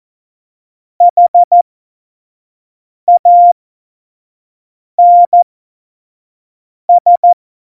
Morse code, heard as a keyed tone, about 14 wpm, spaced as for 5 wpm, 700 Hz. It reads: HANS
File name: Morsekurs / Witze / Witz 29. Januar 2019, 25 F70